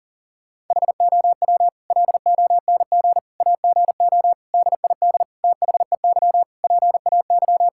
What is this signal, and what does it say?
Morse code, heard as a keyed tone, about 40 wpm, 700 Hz. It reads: HOW LONG AGO DID THEY PAY
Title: Sentences from Top 1000 Words - Review Part 5 - 40wpm